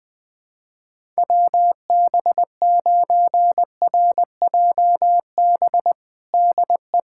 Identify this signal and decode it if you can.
Morse code, heard as a keyed tone, about 20 wpm, 700 Hz. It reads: WB9RJB DE
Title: USA Callsign DE Callsign 20wpm